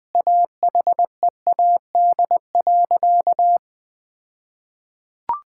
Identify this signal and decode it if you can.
Morse code, heard as a keyed tone, about 20 wpm, 700 Hz. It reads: AHEAD.
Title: All-Age Appropriate Jokes - 20wpm